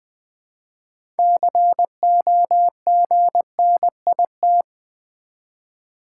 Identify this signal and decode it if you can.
Morse code, heard as a keyed tone, about 20 wpm, 700 Hz. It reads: COGNIT
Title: Build A Word - 3 Letters At A Time - 9-Letter Words - 20wpm